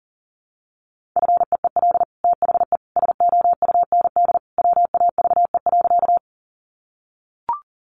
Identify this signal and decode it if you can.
Morse code, heard as a keyed tone, about 40 wpm, 700 Hz. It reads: FEEL THE SOUND WAVE.
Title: Sentences from Top 1400 Words 40wpm